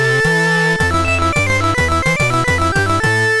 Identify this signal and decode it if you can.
There is no code.